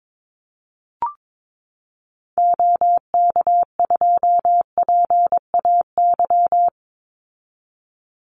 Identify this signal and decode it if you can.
Morse code, heard as a keyed tone, about 22 wpm, 700 Hz. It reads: OK2PAY
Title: Call Signs Repeated 3x - 22wpm